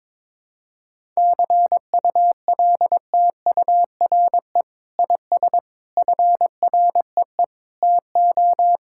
Morse code, heard as a keyed tone, about 22 wpm, 700 Hz. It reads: CULTURE IS FREE TO
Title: Sentences from Top 1000 Words - Review Part 6 - 22wpm